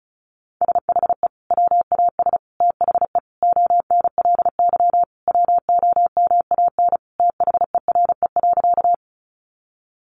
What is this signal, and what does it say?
Morse code, heard as a keyed tone, about 34 wpm, 700 Hz. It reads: SHE WAS THE ONLY WOMAN THERE.
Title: Sentences from Top 1100 Words 35wpm